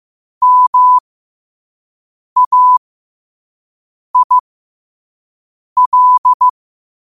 Morse code, heard as a keyed tone, about 15 wpm, spaced as for 5 wpm, 1000 Hz. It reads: MAIL
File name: qotdfiles / 5wpm